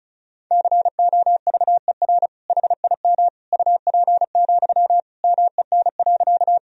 Morse code, heard as a keyed tone, about 36 wpm, 700 Hz. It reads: COVER HIM UP, MEN.
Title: Sentences from Top 1000 Words - Review Part 1 - 35wpm